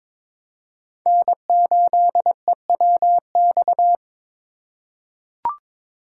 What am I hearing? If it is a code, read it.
Morse code, heard as a keyed tone, about 22 wpm, 700 Hz. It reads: N8EWX
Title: K1USN SST QSO Practice - 22wpm